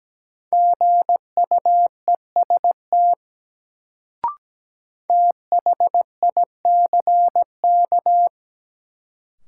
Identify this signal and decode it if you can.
Morse code, heard as a keyed tone, about 17 wpm, 700 Hz. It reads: GUEST THICK
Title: Sentences from Top 1600 Words - New Word Drill - 17wpm